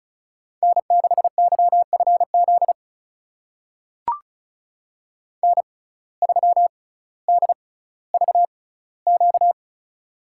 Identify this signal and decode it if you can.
Morse code, heard as a keyed tone, about 36 wpm, spaced as for 18 wpm, 700 Hz. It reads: N6YFZ N3DVQ
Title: USA Call Signs - 5 Characters - 6x Character Spacing - 35wpm